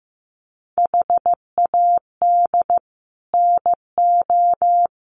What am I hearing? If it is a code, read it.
Morse code, heard as a keyed tone, about 15 wpm, 700 Hz. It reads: HAD NO
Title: Sets of 5 Words 15wpm